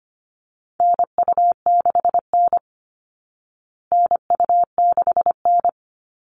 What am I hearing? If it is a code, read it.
Morse code, heard as a keyed tone, about 25 wpm, 700 Hz. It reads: NU6N NU6N
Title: Call Signs Repeated 3x - 25wpm